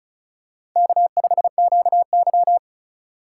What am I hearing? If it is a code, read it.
Morse code, heard as a keyed tone, about 36 wpm, 700 Hz. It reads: K5QY
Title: USA Call Signs - 4x Character Spacing - 35wpm